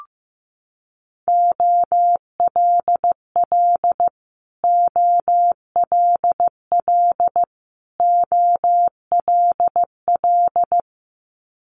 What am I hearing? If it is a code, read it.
Morse code, heard as a keyed tone, about 15 wpm, 700 Hz. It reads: OLL OLL OLL
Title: Multi-letter Phonemes Repeated 3x 15wpm